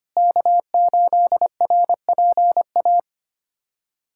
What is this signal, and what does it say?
Morse code, heard as a keyed tone, about 25 wpm, 700 Hz. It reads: K8RPA
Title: USA Callsign DE Callsign - 5 Characters - 25wpm